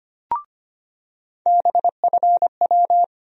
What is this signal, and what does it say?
Morse code, heard as a keyed tone, about 25 wpm, 700 Hz. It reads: BFW
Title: USA Call Sign Postfixes - 3 Letters 25wpm